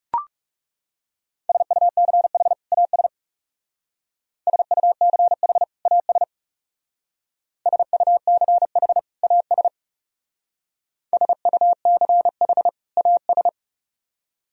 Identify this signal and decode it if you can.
Morse code, heard as a keyed tone, about 39 wpm, 700 Hz. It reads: SUCH AS SUCH AS SUCH AS SUCH AS